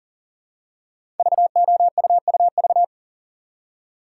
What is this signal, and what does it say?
Morse code, heard as a keyed tone, about 39 wpm, 700 Hz. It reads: VOUUV
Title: Code Groups - 5 Letters - 40wpm